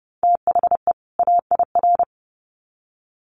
Morse code, heard as a keyed tone, about 30 wpm, 700 Hz. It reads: THE AIR
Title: Sets of 2 Words 30wpm